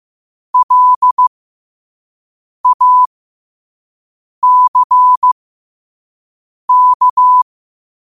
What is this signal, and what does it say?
Morse code, heard as a keyed tone, about 15 wpm, spaced as for 5 wpm, 1000 Hz. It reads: LACK
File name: qotdfiles / 5wpm